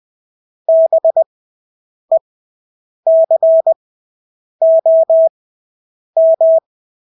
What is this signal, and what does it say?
Morse code, heard as a keyed tone, about 20 wpm, spaced as for 8 wpm, 650 Hz. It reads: BECOM
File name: Engelske ord 20/8 wpm